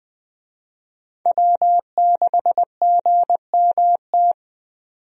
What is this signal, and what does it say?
Morse code, heard as a keyed tone, about 21 wpm, 700 Hz. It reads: W6GMT
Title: USA Call Signs - 5 Characters - 20wpm